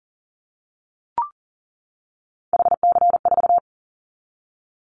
Morse code, heard as a keyed tone, about 38 wpm, 700 Hz. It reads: HC4
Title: Callsign Prefixes for Top 100 DXCC - 40wpm